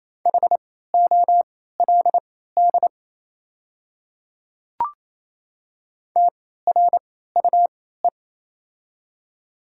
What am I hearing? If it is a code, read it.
Morse code, heard as a keyed tone, about 28 wpm, spaced as for 16 wpm, 700 Hz. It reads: HOLD TRUE
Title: 3x Character Spacing - Top 500 Words - 28wpm